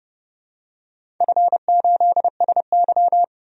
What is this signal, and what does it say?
Morse code, heard as a keyed tone, about 30 wpm, 700 Hz. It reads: F8SY
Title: DXCC-100 Call Signs - 4 Characters - 30wpm